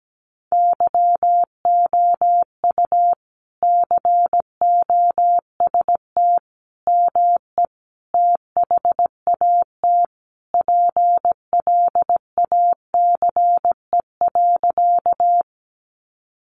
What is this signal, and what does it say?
Morse code, heard as a keyed tone, about 17 wpm, 700 Hz. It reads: YOU COST ME THAT PLACE.
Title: Sentences from Top 800 Words 17wpm